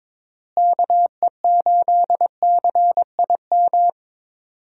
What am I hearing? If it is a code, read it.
Morse code, heard as a keyed tone, about 22 wpm, 700 Hz. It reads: KE8CIM
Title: USA Call Signs - 6 Characters - Repeated 3x - 22wpm